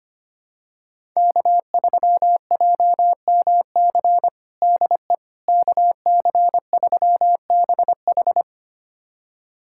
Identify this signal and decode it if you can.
Morse code, heard as a keyed tone, about 25 wpm, 700 Hz. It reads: K3JMC DE KC3BH
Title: USA Callsign DE Callsign - 5 Characters - 25wpm